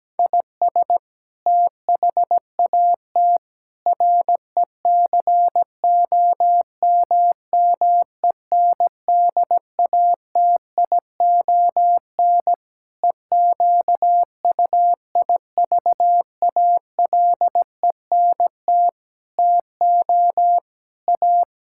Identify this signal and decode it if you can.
Morse code, heard as a keyed tone, about 17 wpm, 700 Hz. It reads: IS THAT RECOMMENDATION EQUIVALENT TO A
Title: Sentences from Top 1600 Words 17wpm